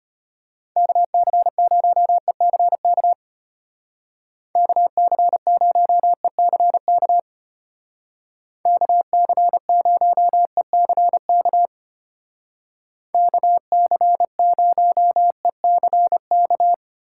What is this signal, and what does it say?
Morse code, heard as a keyed tone, about 31 wpm, 700 Hz. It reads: KC0ECK KC0ECK KC0ECK KC0ECK